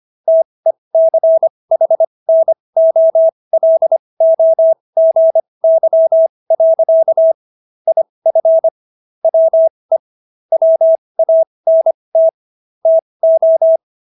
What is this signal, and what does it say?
Morse code, heard as a keyed tone, about 25 wpm, spaced as for 20 wpm, 650 Hz. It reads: TECHNOLOGY. IF WE WANT TO